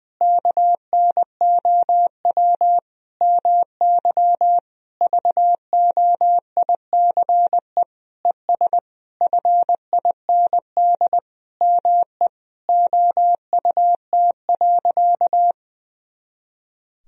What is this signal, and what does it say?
Morse code, heard as a keyed tone, about 20 wpm, 700 Hz. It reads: KNOW MY VOICE ES FIND ME OUT.
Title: Sentences from Top 500 Words and Abbreviations 20wpm